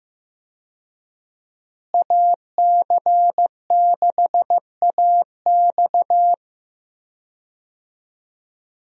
Morse code, heard as a keyed tone, about 15 wpm, 700 Hz. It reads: AC6AX
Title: USA Call Signs - 5 Characters - Repeated 3x - 15wpm